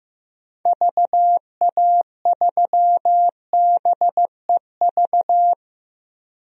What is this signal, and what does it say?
Morse code, heard as a keyed tone, about 15 wpm, 700 Hz. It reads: VA3BEV
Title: DXCC-100 Call Signs - 6 Characters - Repeated 3x - 15wpm